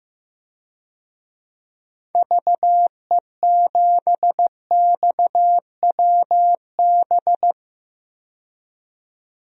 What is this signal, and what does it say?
Morse code, heard as a keyed tone, about 15 wpm, 700 Hz. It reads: VE7XWB
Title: DXCC-100 Call Signs - 6 Characters - 15wpm